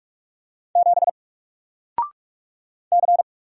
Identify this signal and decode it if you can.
Morse code, heard as a keyed tone, about 44 wpm, 700 Hz. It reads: 7 C
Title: Single Letter-Number Alt 45wpm